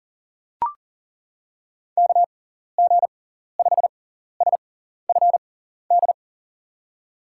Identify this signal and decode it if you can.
Morse code, heard as a keyed tone, about 40 wpm, spaced as for 14 wpm, 700 Hz. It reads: KG5SFD
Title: USA Call Signs - 6 Characters - 6x Character Spacing - 40wpm